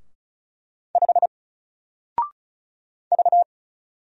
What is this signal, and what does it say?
Morse code, heard as a keyed tone, about 36 wpm, 700 Hz. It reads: 5 V
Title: Single Character (Letter:Number:Punctuation:Prosign) 35wpm